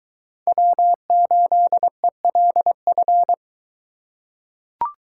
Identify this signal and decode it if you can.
Morse code, heard as a keyed tone, about 21 wpm, 700 Hz. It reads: W8ELF